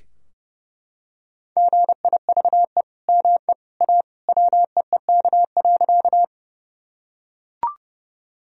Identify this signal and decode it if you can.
Morse code, heard as a keyed tone, about 29 wpm, 700 Hz. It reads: GIVE ME A WEEK.
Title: Sentences from Top 1000 Words - Review Part 2 - 30wpm